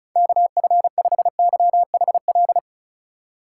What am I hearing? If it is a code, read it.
Morse code, heard as a keyed tone, about 34 wpm, 700 Hz. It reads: KF5YHL
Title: USA Call Signs - 6 Characters - 2x Character Spacing - 35wpm